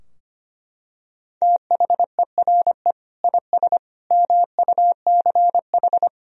Morse code, heard as a keyed tone, about 25 wpm, 700 Hz. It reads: THERE IS MUCH
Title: Sentences from Top 1000 Words - Review Part 4 - 25wpm